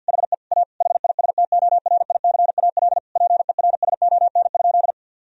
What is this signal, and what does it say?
Morse code, heard as a keyed tone, about 50 wpm, 700 Hz. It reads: HE A HISTORICAL PERSON?
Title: Sentences from Top 1000 Words - Review Part 4 - 50wpm